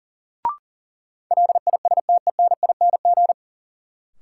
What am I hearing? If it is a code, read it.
Morse code, heard as a keyed tone, about 40 wpm, 700 Hz. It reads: LISTENING
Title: Sets of 5 Words - New Word Drill - 40wpm